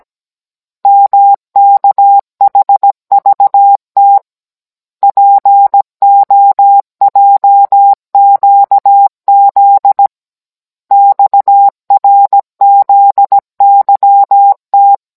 Morse code, heard as a keyed tone, about 17 wpm, 800 Hz. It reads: MKHVT POJQZ XRZYT